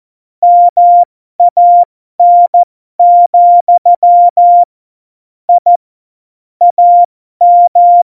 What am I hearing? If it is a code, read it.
Morse code, heard as a keyed tone, about 14 wpm, spaced as for 12 wpm, 700 Hz. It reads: MAN, I AM